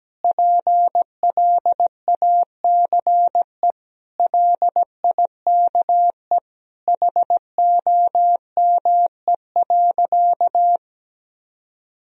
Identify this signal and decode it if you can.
Morse code, heard as a keyed tone, about 17 wpm, 700 Hz. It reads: PLACE LIKE HOME.
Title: Sentences from Top 200 Words and Abbreviations 17wpm